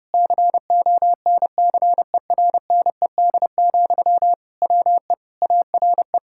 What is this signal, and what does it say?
Morse code, heard as a keyed tone, about 30 wpm, 700 Hz. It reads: CONCERNED, WE ARE
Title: Sentences from Top 1000 Words - Review Part 6 - 30wpm